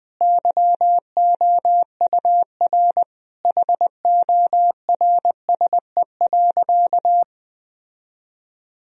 Morse code, heard as a keyed tone, about 20 wpm, 700 Hz. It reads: YOUR HORSE.
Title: Sentences from Top 1000 Words - Review Part 2 - 20wpm